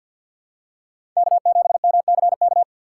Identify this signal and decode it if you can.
Morse code, heard as a keyed tone, about 46 wpm, 700 Hz. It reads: K7MCK
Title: USA Call Signs - 5 Characters - 4x Character Spacing - 50wpm